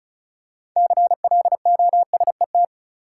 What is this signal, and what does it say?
Morse code, heard as a keyed tone, about 34 wpm, 700 Hz. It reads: CLOSET
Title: Six Letter Words - 35wpm